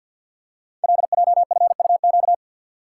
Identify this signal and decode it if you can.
Morse code, heard as a keyed tone, about 50 wpm, 700 Hz. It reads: LJFUQ
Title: Code Groups - 5 Letters - 50wpm